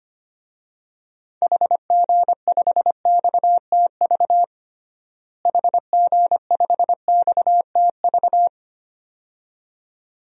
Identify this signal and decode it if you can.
Morse code, heard as a keyed tone, about 25 wpm, 700 Hz. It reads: HG5XTV HG5XTV
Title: DXCC-100 Call Signs - 6 Characters - Repeated 2x - 25wpm